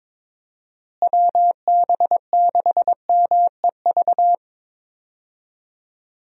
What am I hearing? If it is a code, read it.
Morse code, heard as a keyed tone, about 22 wpm, 700 Hz. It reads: WB6MEV
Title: USA Call Signs - 6 Characters - Repeated 3x - 22wpm